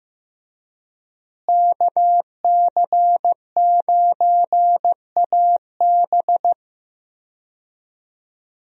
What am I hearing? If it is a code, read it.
Morse code, heard as a keyed tone, about 15 wpm, 700 Hz. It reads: KC9AB
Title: USA Call Signs - 5 Characters - Repeated 3x - 15wpm